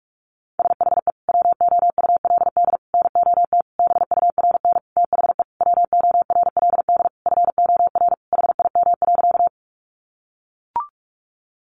Morse code, heard as a keyed tone, about 44 wpm, 700 Hz. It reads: SHE WOULD NOT BURN THE WORLD FOR HIM.
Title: Sentences from Top 1300 Words 45wpm